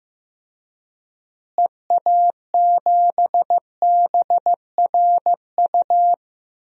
Morse code, heard as a keyed tone, about 15 wpm, 700 Hz. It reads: EA7BRU
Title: DXCC-100 Call Signs - 6 Characters - Repeated 3x - 15wpm